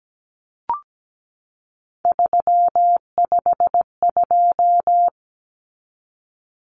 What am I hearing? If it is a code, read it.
Morse code, heard as a keyed tone, about 17 wpm, 700 Hz. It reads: 352
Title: Numbers - 3-Digit 17wpm